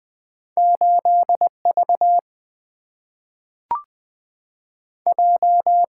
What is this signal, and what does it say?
Morse code, heard as a keyed tone, about 20 wpm, 700 Hz. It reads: 8 V J
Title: Two Letter-Number Combinations - 20wpm